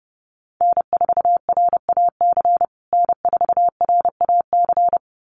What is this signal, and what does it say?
Morse code, heard as a keyed tone, about 30 wpm, 700 Hz. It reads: N4RAC N4RAC
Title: USA Call Signs Repeated 2x 30wpm